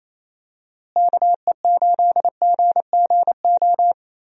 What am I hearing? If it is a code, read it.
Morse code, heard as a keyed tone, about 27 wpm, 700 Hz. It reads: KE8GGO
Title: USA Call Signs - 6 Characters - Repeated 2x - 28wpm